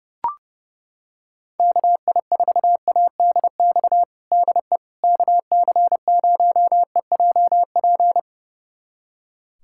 Morse code, heard as a keyed tone, about 30 wpm, 700 Hz. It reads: KI4ADX DE KC0EJP
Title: USA Callsign DE Callsign - 6 Characters - 30wpm